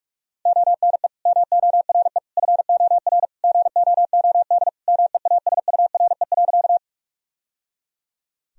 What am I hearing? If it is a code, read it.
Morse code, heard as a keyed tone, about 46 wpm, 700 Hz. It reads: ONE MORE FOR GOOD MEASURE.
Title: Sentences from Top 1000 Words - Review Part 2 - 45wpm